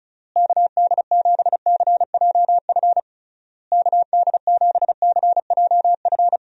Morse code, heard as a keyed tone, about 36 wpm, 700 Hz. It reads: KD7CJF KD7CJF
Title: USA Call Signs - 6 Characters - Repeated 3x - 35wpm